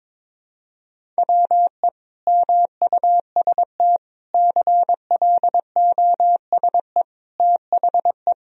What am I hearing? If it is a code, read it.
Morse code, heard as a keyed tone, about 21 wpm, 700 Hz. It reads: WE MUST CLOSE THE
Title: Sentences from Top 1000 Words - Review Part 1 - 22wpm